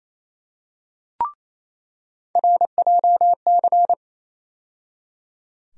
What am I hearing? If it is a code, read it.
Morse code, heard as a keyed tone, about 28 wpm, 700 Hz. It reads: RJC
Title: USA Call Sign Postfixes - 3 Letters 28wpm